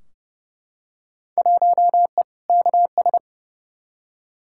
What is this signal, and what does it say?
Morse code, heard as a keyed tone, about 30 wpm, 700 Hz. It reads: 1E KS
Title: Field Day Exchanges - 30wpm